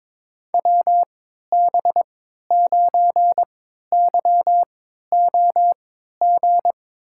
Morse code, heard as a keyed tone, about 22 wpm, spaced as for 12 wpm, 700 Hz. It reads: WB9YOG
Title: USA Call Signs - 6 Characters - 3x Character Spacing - 22wpm